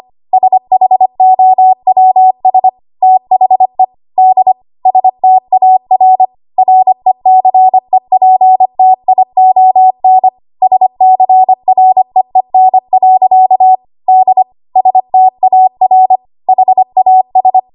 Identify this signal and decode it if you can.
Morse code, heard as a keyed tone, about 25 wpm, 750 Hz. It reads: SHOWS THE D STAR RECEPTION SCREEN. D STAR HAS
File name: Archive / 25 WPM / 25WPM